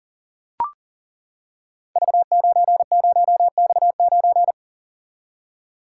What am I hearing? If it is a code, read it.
Morse code, heard as a keyed tone, about 38 wpm, 690 Hz. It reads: V90X9
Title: Code Groups - 5 Letter:Numbers - 40wpm